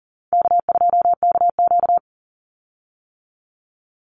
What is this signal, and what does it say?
Morse code, heard as a keyed tone, about 38 wpm, 700 Hz. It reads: K2KQ
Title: Call Signs - 40wpm